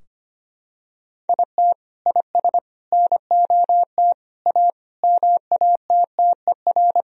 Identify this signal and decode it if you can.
Morse code, heard as a keyed tone, about 25 wpm, 700 Hz. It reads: IT IS NOT A MATTER
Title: Sentences from Top 1000 Words - Review Part 4 - 25wpm